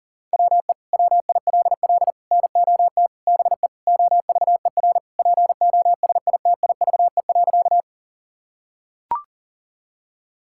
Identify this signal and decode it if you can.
Morse code, heard as a keyed tone, about 40 wpm, 700 Hz. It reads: WE WILL NOT BE OVER POSITIVE.
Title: Sentences from Top 1000 Words - Review Part 5 - 40wpm